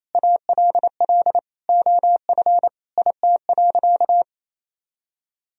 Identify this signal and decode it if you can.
Morse code, heard as a keyed tone, about 28 wpm, 700 Hz. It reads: ALL OF IT.
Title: Sentences from Top 500 Words and Abbreviations 28wpm